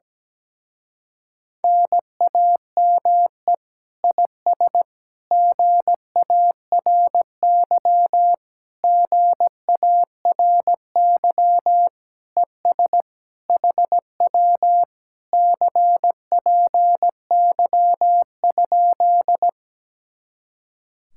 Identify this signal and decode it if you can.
Morse code, heard as a keyed tone, about 17 wpm, 700 Hz. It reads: NAME IS GARY GARY ES HW CPY?
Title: Speed-Racing Common QSO Phrases Encore - 17wpm